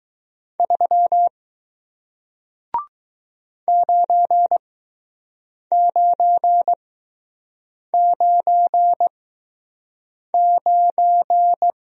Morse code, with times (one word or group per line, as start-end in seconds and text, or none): 0.6-1.3 3
3.7-11.7 9999